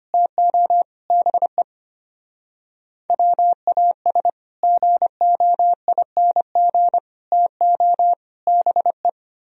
Morse code, heard as a keyed tone, about 32 wpm, 700 Hz. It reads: TO BE WAS GOING TO BE